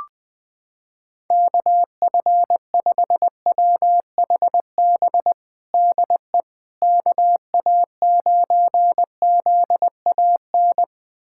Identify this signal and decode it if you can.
Morse code, heard as a keyed tone, about 20 wpm, 700 Hz. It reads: KF5WHB DE KA9ZAN
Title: USA Callsign DE Callsign - 6 Characters - 20wpm